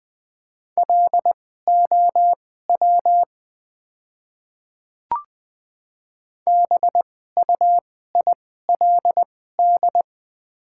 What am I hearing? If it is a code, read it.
Morse code, heard as a keyed tone, about 20 wpm, spaced as for 15 wpm, 700 Hz. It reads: LOW BUILD
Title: 2x Character Spacing - Top 300 Words - 20wpm